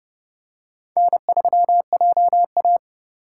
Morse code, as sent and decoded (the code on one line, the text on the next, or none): -. ...-- .--- .-
N3JA